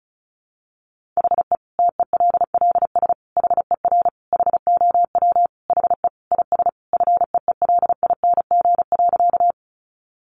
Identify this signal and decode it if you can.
Morse code, heard as a keyed tone, about 34 wpm, 700 Hz. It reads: HE TELLS HER HOW HE IS FEELING.
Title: Additional Easy Phrases 35wpm